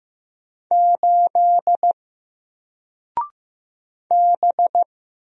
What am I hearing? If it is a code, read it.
Morse code, heard as a keyed tone, about 15 wpm, 700 Hz. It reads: 8 B
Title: Single Letter-Number Alt 15wpm